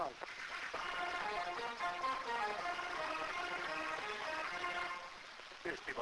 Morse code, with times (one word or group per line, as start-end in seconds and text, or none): none